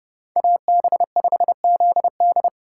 Morse code, heard as a keyed tone, about 30 wpm, 700 Hz. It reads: AB5ZD
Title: Speed Racing - USA 2x2 Call Signs 25wpm